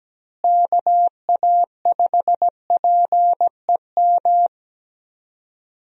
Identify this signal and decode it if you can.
Morse code, heard as a keyed tone, about 17 wpm, 700 Hz. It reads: KA5PEM
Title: USA Call Signs - 6 Characters - Repeated 2x - 17wpm